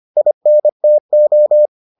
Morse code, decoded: INTO

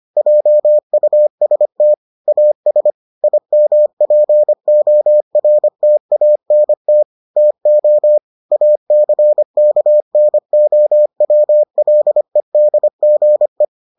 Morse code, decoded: JUST AS IMPORTANT TO ACKNOWLEDGE

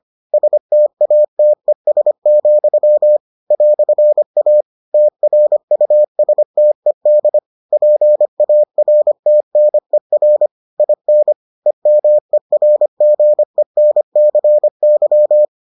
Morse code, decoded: STATES, "A TRUSTED PARTNER IN EMERGENCY

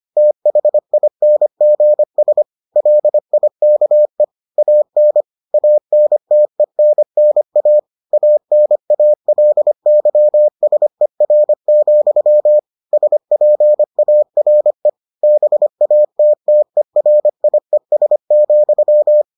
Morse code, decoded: THINGS LIKE AN ANTENNA ANALYSER, SPARE BATTERIES,

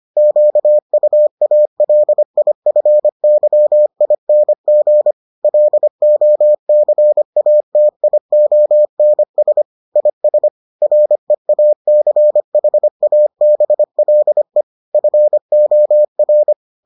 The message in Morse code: QUALIFYING LOCATIONS IS REACHABLE FOR

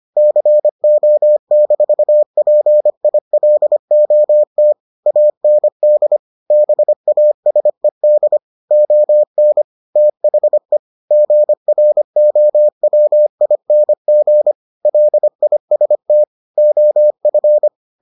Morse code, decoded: CO-PILOT AND BASED ON THE GROWING LIST OF

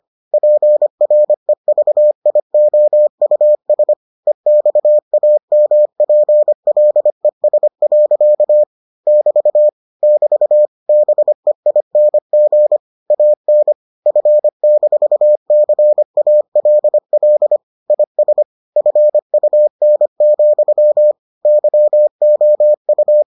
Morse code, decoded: PREVIOUS EXAMPLES. <BT> <BT> BEING AN F-CALL IS FUN, YOU

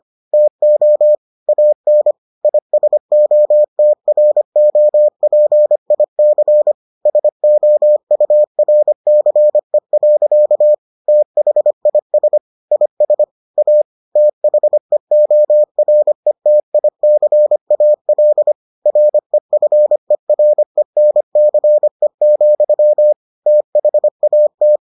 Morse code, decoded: TO AN ISOTROPIC SOURCE. THIS IS A THEORETICAL REFERENCE, THAT

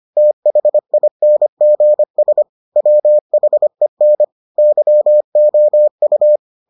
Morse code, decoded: THINGS WHEN YOU

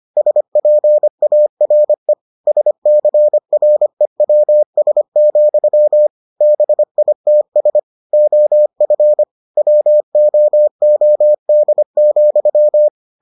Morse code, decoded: SPARE SCREWS, BITS OF WOOD,

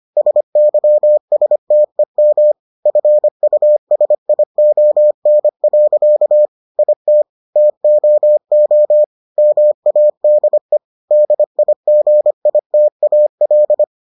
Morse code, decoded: SYSTEM FUSION. IT TOO MADE DIGITAL

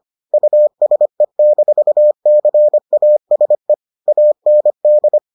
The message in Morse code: USE-CASE AND

